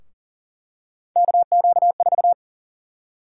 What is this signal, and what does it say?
Morse code, heard as a keyed tone, about 40 wpm, 700 Hz. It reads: KQ4
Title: USA Call Sign Prefixes - 40wpm